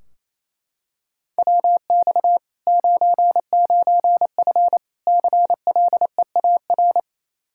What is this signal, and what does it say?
Morse code, heard as a keyed tone, about 28 wpm, 700 Hz. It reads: WX 99F CLEAR
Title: Practice QSO - Exchange 2 - Info - 28wpm